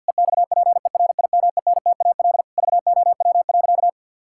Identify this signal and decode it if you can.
Morse code, heard as a keyed tone, about 50 wpm, 700 Hz. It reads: EXPERIMENTAL VOW.
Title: Pangram Challenge v2 - 50wpm